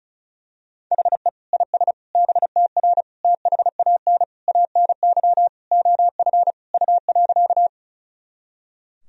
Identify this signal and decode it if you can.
Morse code, heard as a keyed tone, about 36 wpm, 700 Hz. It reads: HE IS BTR THAN ANY OF U.
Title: Sentences from Top 400 Words and Abbreviations 35wpm